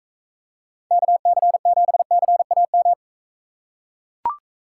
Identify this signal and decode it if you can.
Morse code, heard as a keyed tone, about 42 wpm, 700 Hz. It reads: KC7CAM